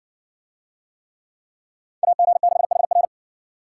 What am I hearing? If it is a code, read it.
Morse code, heard as a keyed tone, about 60 wpm, 700 Hz. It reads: AG6HR